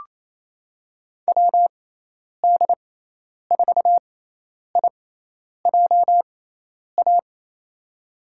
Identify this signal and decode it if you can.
Morse code, heard as a keyed tone, about 28 wpm, spaced as for 10 wpm, 700 Hz. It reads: WD4IJA
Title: USA Call Signs - 6 Characters - 6x Character Spacing - 28wpm